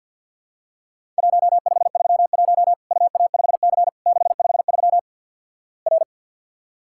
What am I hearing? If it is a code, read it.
Morse code, heard as a keyed tone, about 50 wpm, 700 Hz. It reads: 1531 UA5C 653 R
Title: Worked All Europe DX Contest - QTC Exchange with Repeats - 50wpm